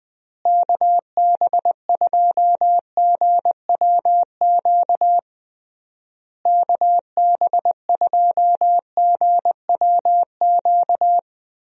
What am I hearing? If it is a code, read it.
Morse code, heard as a keyed tone, about 21 wpm, 700 Hz. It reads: KB2GWQ KB2GWQ
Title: USA Call Signs - 6 Characters - Repeated 3x - 20wpm